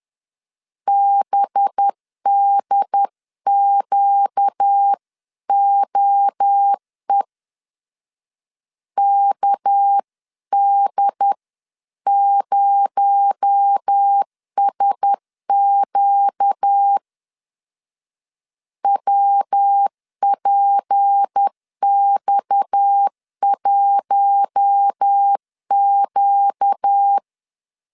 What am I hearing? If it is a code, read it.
Morse code, heard as a keyed tone, about 10 wpm, 800 Hz. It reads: BDQOE KD0SQ WPX1Q
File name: CW / DET / 1 6 1500C S B 10wpm 4137